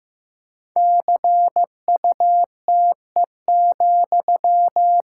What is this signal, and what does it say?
Morse code, heard as a keyed tone, about 15 wpm, 700 Hz. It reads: CUTE,
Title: Pangram Challenge v2 - 15wpm